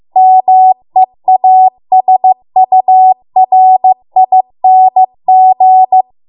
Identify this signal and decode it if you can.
Morse code, heard as a keyed tone, about 15 wpm, 750 Hz. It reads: MEASURING